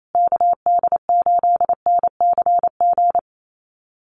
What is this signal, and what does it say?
Morse code, heard as a keyed tone, about 28 wpm, 700 Hz. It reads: KD8NCG
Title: USA Call Signs Repeated 3x 28wpm